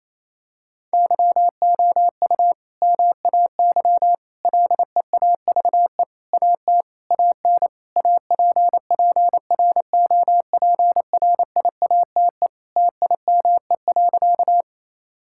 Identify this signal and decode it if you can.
Morse code, heard as a keyed tone, about 28 wpm, 700 Hz. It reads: YOU MAY LEAVE AT AN APPROPRIATE TIME.
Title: Sentences from Top 1000 Words - Review Part 5 - 28wpm